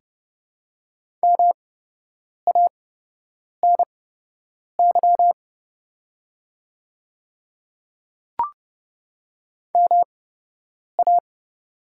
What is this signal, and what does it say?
Morse code, heard as a keyed tone, about 32 wpm, spaced as for 8 wpm, 700 Hz. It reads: MANY MA
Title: 8x Character Spacing - Top 300 Words - 30wpm